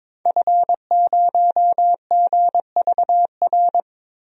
Morse code, heard as a keyed tone, about 22 wpm, 700 Hz. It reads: F0GVR
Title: DXCC-100 Call Signs - 5 Characters - Repeated 3x - 22wpm